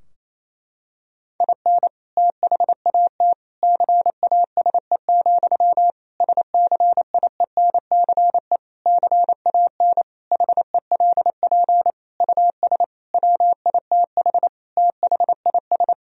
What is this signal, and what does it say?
Morse code, heard as a keyed tone, about 28 wpm, 700 Hz. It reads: IN THAT CASE, SCIENCE CAN HELP US WITH THIS
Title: Sentences from Top 1000 Words - Review Part 2 - 28wpm